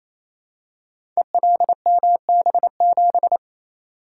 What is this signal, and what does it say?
Morse code, heard as a keyed tone, about 28 wpm, 690 Hz. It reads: ELMB7
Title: Code Groups - 5 Letter:Numbers - 28wpm